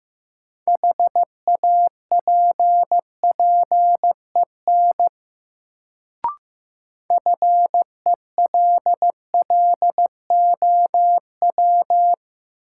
Morse code, heard as a keyed tone, about 15 wpm, 700 Hz. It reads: HAPPEN FELLOW
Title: Sets of 5 Words Encore - New Word Drill - 15wpm